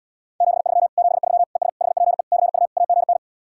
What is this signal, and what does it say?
Morse code, heard as a keyed tone, about 79 wpm, 700 Hz. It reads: 73 73 ES HPE CU AGN